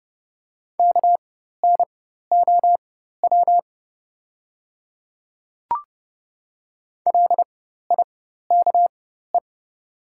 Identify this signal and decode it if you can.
Morse code, heard as a keyed tone, about 30 wpm, spaced as for 14 wpm, 700 Hz. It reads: KNOW LIKE
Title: 4x Character Spacing - Top 100 Words - 30wpm